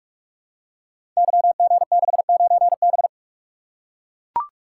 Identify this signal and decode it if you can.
Morse code, heard as a keyed tone, about 44 wpm, 700 Hz. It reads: YG69B
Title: Code Groups - 5 Letter:Numbers - 45wpm